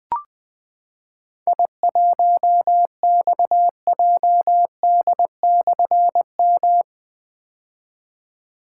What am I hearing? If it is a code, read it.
Morse code, heard as a keyed tone, about 20 wpm, 700 Hz. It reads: I1XJD/M
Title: Difficult Call Signs 20wpm